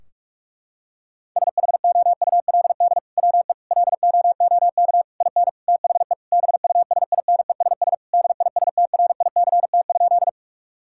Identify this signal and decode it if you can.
Morse code, heard as a keyed tone, about 46 wpm, 700 Hz. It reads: SHOULD WE LOOK IN THE BUSINESS DISTRICT?
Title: Sentences from Top 1600 Words 45wpm